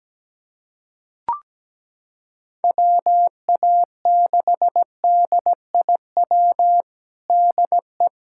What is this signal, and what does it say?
Morse code, heard as a keyed tone, about 17 wpm, 700 Hz. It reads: WA6DIW DE
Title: USA Callsign DE Callsign 17wpm